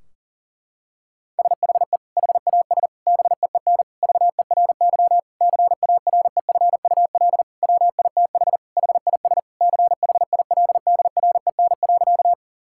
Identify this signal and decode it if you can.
Morse code, heard as a keyed tone, about 40 wpm, 700 Hz. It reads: SHE HAS BEEN VERY CAREFUL WITH HIS CHILDREN.
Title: Sentences from Top 1600 Words 40wpm